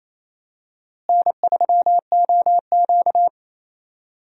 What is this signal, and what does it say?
Morse code, heard as a keyed tone, about 28 wpm, 700 Hz. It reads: N3OQ
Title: USA Call Signs - 4 Characters - Repeated 2x - 28wpm